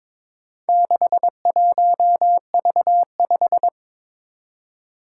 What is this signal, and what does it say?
Morse code, heard as a keyed tone, about 22 wpm, 700 Hz. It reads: 61V5
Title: Four Challenging Letter-Number Combinations 22wpm